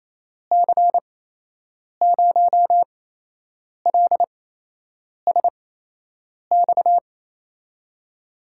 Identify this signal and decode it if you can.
Morse code, heard as a keyed tone, about 28 wpm, spaced as for 8 wpm, 700 Hz. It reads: C0LSX
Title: USA Call Signs - 6 Characters - 8x Character Spacing - 28wpm